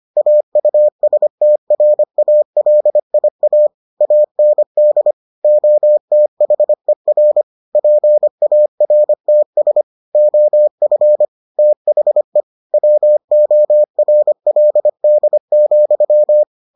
Morse code, heard as a keyed tone, about 25 wpm, 600 Hz. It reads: AUSTRALIA AND OTHER PARTS OF THE WORLD,